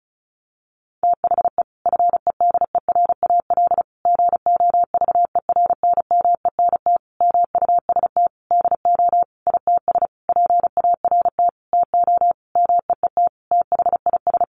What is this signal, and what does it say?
Morse code, heard as a keyed tone, about 34 wpm, 700 Hz. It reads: THE FEDERAL GOVERNMENT MUST DO ITS PART TO MEET THIS
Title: Sentences from Top 1100 Words 35wpm